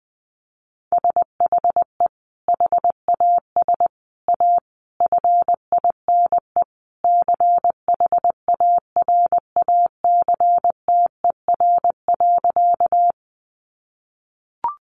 Morse code, heard as a keyed tone, about 20 wpm, 700 Hz. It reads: SHE HAS A FINE CHARACTER.
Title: Sentences from Top 1200 Words 20wpm